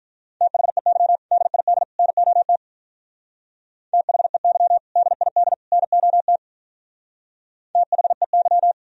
Morse code, heard as a keyed tone, about 42 wpm, 700 Hz. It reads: THEY DID NOT THEY DID NOT THEY
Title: Speed-Racing - Sets of 3 Words 35wpm